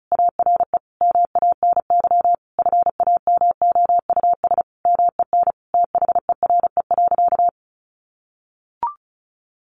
Morse code, heard as a keyed tone, about 34 wpm, 700 Hz. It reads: ARE MANY FAMOUS MEN THERE.
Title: Sentences from Top 1500 Words 35wpm